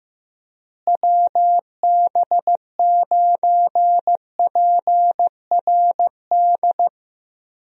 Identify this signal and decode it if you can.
Morse code, heard as a keyed tone, about 15 wpm, 700 Hz. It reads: WB9PRD
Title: USA Call Signs - 6 Characters - Repeated 2x - 15wpm